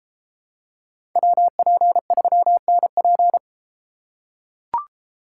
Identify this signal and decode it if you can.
Morse code, heard as a keyed tone, about 32 wpm, 700 Hz. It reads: WP3NP